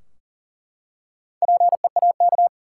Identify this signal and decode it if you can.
Morse code, heard as a keyed tone, about 40 wpm, 700 Hz. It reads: PEAK
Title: Four Letter Words - 40wpm